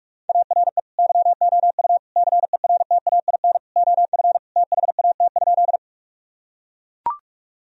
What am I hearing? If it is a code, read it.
Morse code, heard as a keyed tone, about 41 wpm, 700 Hz. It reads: ARE YOU CERTAIN OF THAT?
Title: Sentences from Top 1000 Words - Review Part 2 - 45wpm